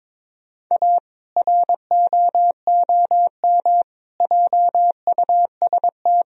Morse code, heard as a keyed tone, about 22 wpm, 700 Hz. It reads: A ROOM JUST
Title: Sentences from Top 1000 Words - Review Part 2 - 22wpm